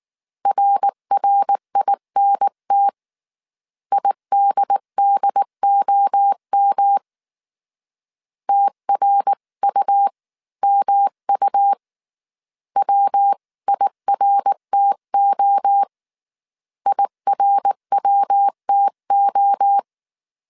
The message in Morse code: RRINT IDDOM TRUMU WIRTO IRWTO

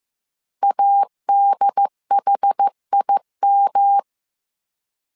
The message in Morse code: ADHIM

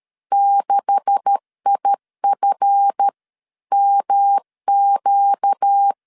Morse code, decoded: 6IFMQ

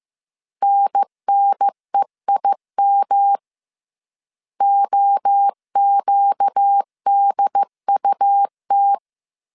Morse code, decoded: NNEIM OQDUT

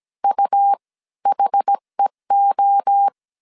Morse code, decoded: UHEO

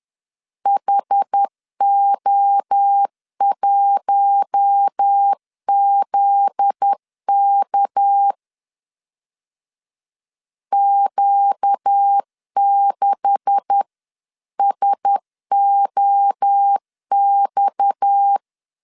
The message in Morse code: HO1ZK Q6SOX

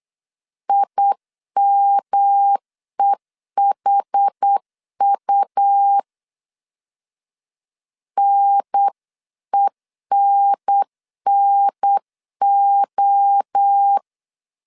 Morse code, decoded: IMEHU NENNO

